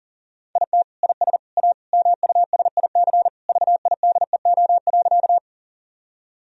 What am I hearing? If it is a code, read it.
Morse code, heard as a keyed tone, about 40 wpm, 700 Hz. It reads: IT IS A MUSIC VIDEO.
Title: Sentences from Top 1000 Words - Review Part 4 - 40wpm